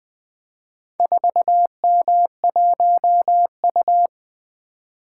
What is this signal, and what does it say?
Morse code, heard as a keyed tone, about 20 wpm, 700 Hz. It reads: 4M1U
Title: Four Letter-Number Combinations - 20wpm